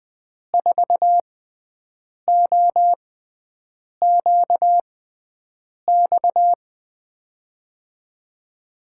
Morse code, heard as a keyed tone, about 20 wpm, spaced as for 7 wpm, 700 Hz. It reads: 4OQX
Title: USA Call Signs - 6 Characters - 6x Character Spacing - 20wpm